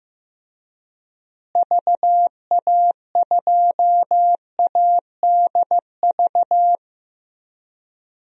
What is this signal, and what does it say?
Morse code, heard as a keyed tone, about 15 wpm, 700 Hz. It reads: VA2ADV
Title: DXCC-100 Call Signs - 6 Characters - Repeated 3x - 15wpm